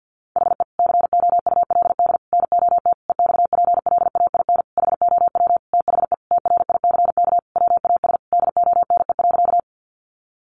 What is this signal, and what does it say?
Morse code, heard as a keyed tone, about 50 wpm, 700 Hz. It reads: HE COULD NOT EXPLAIN HOW THE TRICK WAS DONE.
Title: Sentences from Top 1500 Words 50wpm